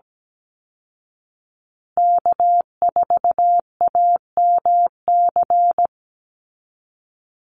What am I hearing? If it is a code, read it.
Morse code, heard as a keyed tone, about 17 wpm, 700 Hz. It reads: K4AMC
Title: Call Signs Repeated 2x - 17wpm